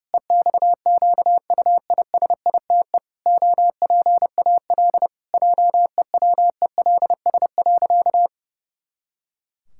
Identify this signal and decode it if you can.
Morse code, heard as a keyed tone, about 30 wpm, 700 Hz. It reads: EXQUISITE OPAL JEWELS.
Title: Pangram Challenge v2 - 30wpm